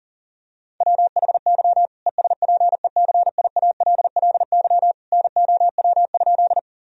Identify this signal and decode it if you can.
Morse code, heard as a keyed tone, about 40 wpm, 700 Hz. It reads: WHY ESPECIALLY NOW?
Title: Sentences from Top 1000 Words - Review Part 4 - 40wpm